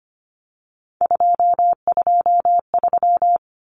Morse code, heard as a keyed tone, about 25 wpm, 700 Hz. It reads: 223
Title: Common Cut Numbers - 3-Digit 25wpm